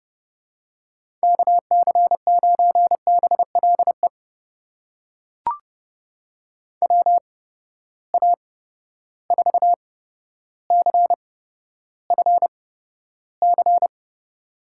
Morse code, heard as a keyed tone, about 30 wpm, spaced as for 13 wpm, 700 Hz. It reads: KC9BLE WA4CFC
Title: USA Call Signs - 6 Characters - 8x Character Spacing - 30wpm